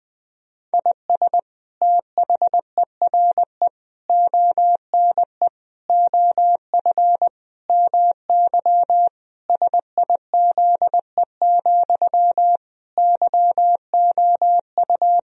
Morse code, heard as a keyed tone, about 20 wpm, 700 Hz. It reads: IS THERE ONE OF MY SIZE, YOU
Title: Sentences from Top 1000 Words - Review Part 2 - 20wpm